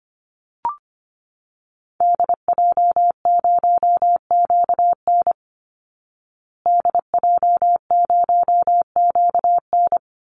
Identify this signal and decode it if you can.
Morse code, heard as a keyed tone, about 25 wpm, 700 Hz. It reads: DJ0QN DJ0QN
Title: Call Signs Repeated 3x - 25wpm